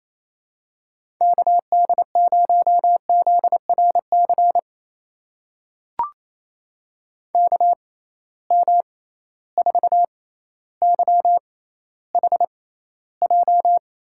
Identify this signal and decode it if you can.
Morse code, heard as a keyed tone, about 28 wpm, spaced as for 14 wpm, 700 Hz. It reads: KD0ZRC KM4YHJ
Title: USA Call Signs - 6 Characters - 6x Character Spacing - 28wpm